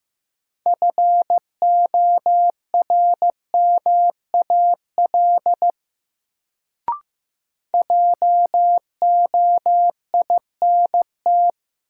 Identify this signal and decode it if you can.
Morse code, heard as a keyed tone, about 16 wpm, 700 Hz. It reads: FORMAL JOINT
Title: Sentences from Top 1300 Words - New Word Drill Alt - 15wpm